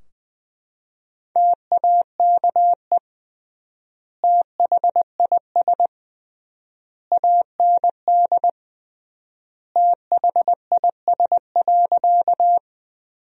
Morse code, heard as a keyed tone, about 20 wpm, 700 Hz. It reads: TAKE THIS AND THIS.
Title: 3x Word-Spacing - Sentences from Top 200 Words - 20wpm